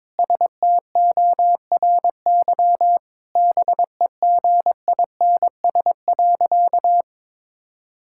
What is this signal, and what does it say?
Morse code, heard as a keyed tone, about 22 wpm, 700 Hz. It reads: STORY BEGINS.
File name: practice / 22wpm / Sentences with 2-Word Phrases - 22wpm